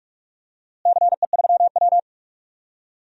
Morse code, heard as a keyed tone, about 46 wpm, 700 Hz. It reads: CE3W
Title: DXCC-100 Call Signs - 4 Characters - Repeated 3x - 45wpm